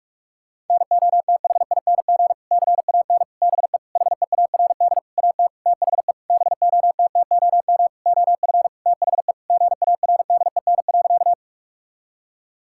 Morse code, heard as a keyed tone, about 46 wpm, 700 Hz. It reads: NOTHING CAN BE HEARD AT THE BOTTOM OF THE GARDEN.
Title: Sentences from Top 1000 Words - Review Part 6 - 45wpm